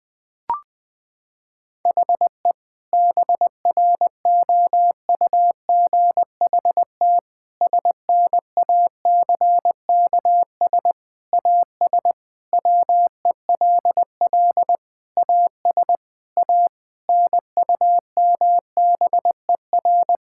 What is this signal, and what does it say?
Morse code, heard as a keyed tone, about 20 wpm, 700 Hz. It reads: HE BROUGHT SNACKS AS WELL AS A NUMBER
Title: Sentences with 6-Word Phrases - 20wpm